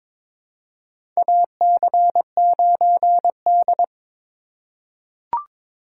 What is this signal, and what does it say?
Morse code, heard as a keyed tone, about 22 wpm, 700 Hz. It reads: AC9D
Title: USA Call Signs - 4x Character Spacing - 22wpm